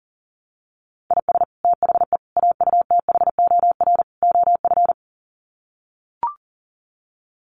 Morse code, heard as a keyed tone, about 40 wpm, 700 Hz. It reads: IS THE AUTHOR OF
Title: Sets of 4 Words 40wpm